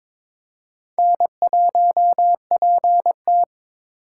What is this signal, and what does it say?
Morse code, heard as a keyed tone, about 22 wpm, 700 Hz. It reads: N1PT
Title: USA Call Signs - 4 Characters - 3x Character Spacing - 22wpm